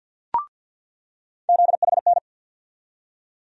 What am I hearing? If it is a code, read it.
Morse code, heard as a keyed tone, about 50 wpm, 700 Hz. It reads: ZHN
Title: USA Call Sign Postfixes - 3 Letters 50wpm